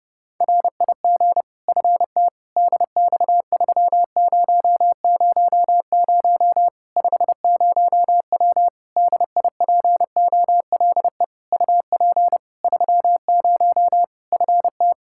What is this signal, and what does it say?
Morse code, heard as a keyed tone, about 30 wpm, 700 Hz. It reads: RIG FT DX3000 50W DIPOLE UP 30 FT
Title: Practice QSO - Exchange 2 - Info - 30wpm